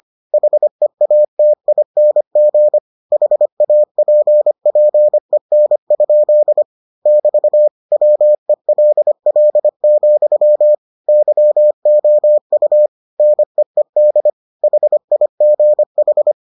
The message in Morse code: HEATING HAPPEN? <BT> WELL, YOU NEED HIGH

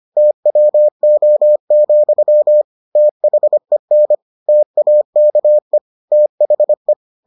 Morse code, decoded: TWO, THEN TAKE THE

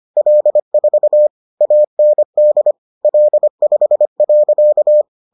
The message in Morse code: L4 AND L5.

